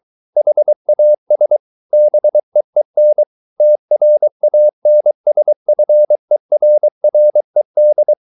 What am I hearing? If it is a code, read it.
Morse code, heard as a keyed tone, about 23 wpm, 600 Hz. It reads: HAS BEEN TRANSFERRED